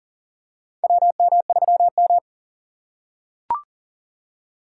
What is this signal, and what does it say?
Morse code, heard as a keyed tone, about 40 wpm, 700 Hz. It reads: WM3M
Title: USA Call Signs - 4 Characters - 3x Character Spacing - 40wpm